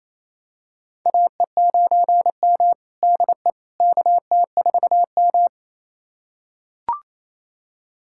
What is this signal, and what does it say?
Morse code, heard as a keyed tone, about 28 wpm, 700 Hz. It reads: AE9M DE KT4M
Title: USA Callsign DE Callsign - 4 Characters - 28wpm